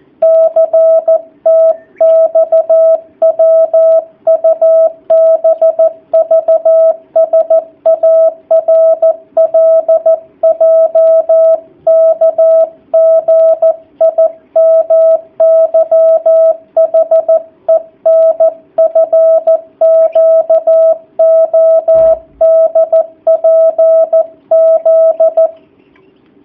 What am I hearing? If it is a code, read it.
Morse code, heard as a keyed tone, about 14 wpm, 650 Hz. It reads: CTXWUBVSARLJKGIMYHENFQODPZ
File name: jh8cbh / 2019cw / sp13- w